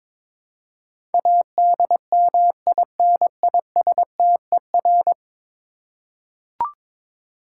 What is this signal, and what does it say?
Morse code, heard as a keyed tone, about 22 wpm, 700 Hz. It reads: ADMINISTER
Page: morse-code-ninja-practice-22wpm